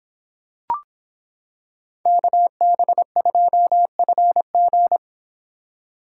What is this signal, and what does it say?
Morse code, heard as a keyed tone, about 26 wpm, 700 Hz. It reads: KB2FG